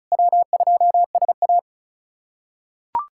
Morse code, heard as a keyed tone, about 34 wpm, 700 Hz. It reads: W2SA
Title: K1USN SST QSO Practice - 35wpm